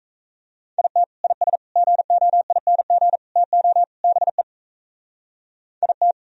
Morse code, heard as a keyed tone, about 35 wpm, 700 Hz. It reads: IT IS GOING TO BE IT
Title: Speed-Racing - Sets of 5 Words 28wpm